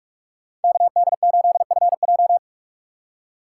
Morse code, heard as a keyed tone, about 45 wpm, 690 Hz. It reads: KD8FJ
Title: USA Call Signs - 5 Characters - 2x Character Spacing - 45wpm